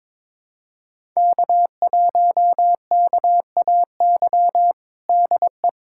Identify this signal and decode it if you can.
Morse code, heard as a keyed tone, about 22 wpm, 700 Hz. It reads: K1KAY DE
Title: USA Callsign DE Callsign - 5 Characters - 22wpm